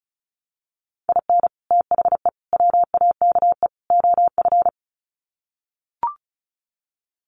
Morse code, heard as a keyed tone, about 34 wpm, 700 Hz. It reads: IN THE WAKE OF
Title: Sets of 4 Words 35wpm